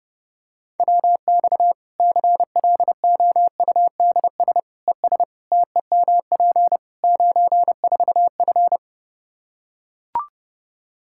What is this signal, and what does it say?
Morse code, heard as a keyed tone, about 30 wpm, 700 Hz. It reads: WX CLOUDS ES TEMP 94F
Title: K1USN SST QSO Practice 30wpm